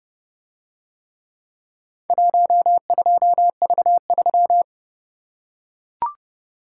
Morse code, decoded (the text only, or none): 12V3